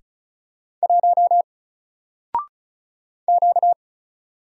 Morse code, heard as a keyed tone, about 34 wpm, 700 Hz. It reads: 1 Q
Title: Single Letter-Number 35wpm